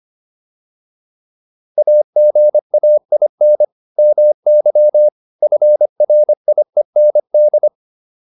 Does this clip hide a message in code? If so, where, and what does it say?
Yes: Morse code from 1.8 to 7.7 s, heard as a keyed tone, about 25 wpm, 600 Hz. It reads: AGAIN MY FRIEND